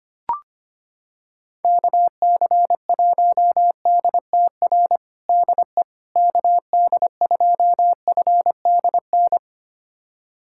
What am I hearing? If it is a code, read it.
Morse code, heard as a keyed tone, about 25 wpm, 700 Hz. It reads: KC1DTR DE KD2FDN
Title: USA Callsign DE Callsign 25wpm